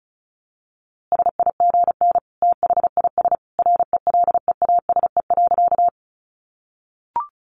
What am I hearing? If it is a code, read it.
Morse code, heard as a keyed tone, about 34 wpm, 700 Hz. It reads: SIGN THIS RELEASE.
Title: A Sound Of Thunder 35wpm